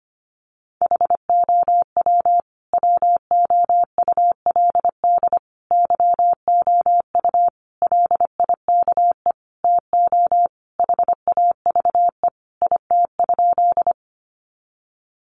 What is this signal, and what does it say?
Morse code, heard as a keyed tone, about 25 wpm, 700 Hz. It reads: HOW WOULD YOU LIKE TO HAVE IT?
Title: Sentences from Top 100 Words 25wpm